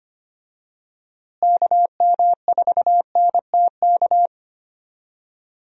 Morse code, heard as a keyed tone, about 25 wpm, 700 Hz. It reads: KM4NTK
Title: USA Call Signs - 6 Characters - Repeated 2x - 25wpm